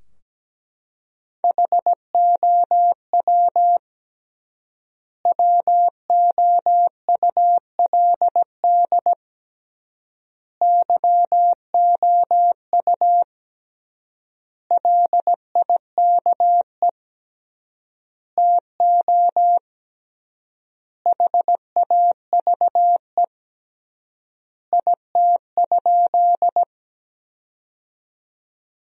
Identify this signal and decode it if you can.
Morse code, heard as a keyed tone, about 17 wpm, 700 Hz. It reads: HOW WOULD YOU LIKE TO HAVE IT?
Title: 3x Word-Spacing - Sentences from Top 100 Words - 17wpm